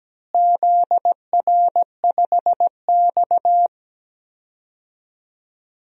Morse code, heard as a keyed tone, about 17 wpm, 700 Hz. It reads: ZR5X
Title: DXCC-100 Call Signs - 4 Characters - Repeated 3x - 17wpm